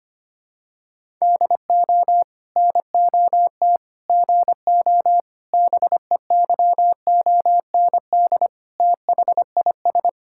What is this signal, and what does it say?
Morse code, heard as a keyed tone, about 25 wpm, 700 Hz. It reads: DO NOT GO BEYOND THIS
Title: Sentences from Top 1000 Words - Review Part 6 - 25wpm